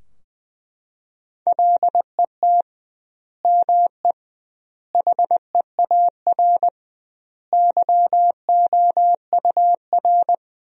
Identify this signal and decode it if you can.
Morse code, heard as a keyed tone, about 20 wpm, 700 Hz. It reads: LET ME HEAR YOUR
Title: 2x Word-Spacing - Sentences from Top 300 Words - 20wpm